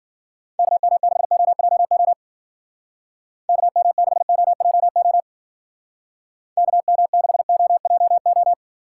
Morse code, heard as a keyed tone, about 49 wpm, 700 Hz. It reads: KM6OJO KM6OJO KM6OJO
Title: Speed Racing - USA 2x3 Call Signs 40wpm